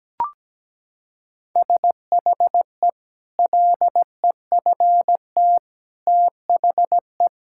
Morse code, heard as a keyed tone, about 17 wpm, 700 Hz. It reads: SHE LEFT THE
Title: Sentences from Top 1000 Words - Review Part 5 - 17wpm